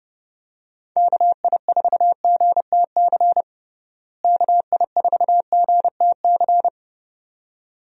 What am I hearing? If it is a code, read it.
Morse code, heard as a keyed tone, about 32 wpm, 700 Hz. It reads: KI4GTC KI4GTC
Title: USA Call Signs - 6 Characters - Repeated 2x - 30wpm